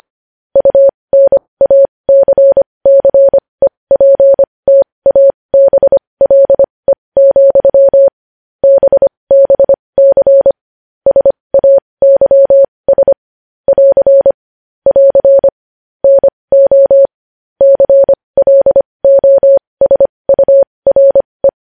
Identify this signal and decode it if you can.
Morse code, heard as a keyed tone, about 25 wpm, spaced as for 20 wpm, 550 Hz. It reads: UNACCEPTABLE, BBC SAYS <AR> <AR> NO CLOSURE
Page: News headlines in Morse code at 20 wpm